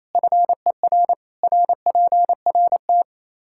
Morse code, heard as a keyed tone, about 28 wpm, 700 Hz. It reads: FER RPRT
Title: Practice QSO - Exchange 2 - Info - 28wpm